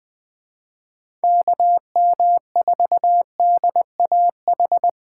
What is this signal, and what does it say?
Morse code, heard as a keyed tone, about 21 wpm, 700 Hz. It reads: KM4DAH